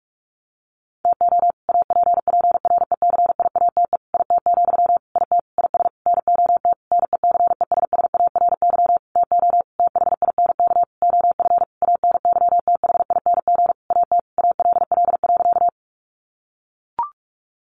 Morse code, heard as a keyed tone, about 44 wpm, 700 Hz. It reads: TO APPRECIATE IT, IT IS NOT NECESSARY TO THINK OF ANYTHING AT ALL.
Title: Sentences from Top 1300 Words 45wpm